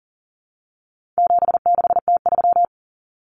Morse code, heard as a keyed tone, about 40 wpm, 700 Hz. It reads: 76T3
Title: Common Cut Numbers - 4-Digit 40wpm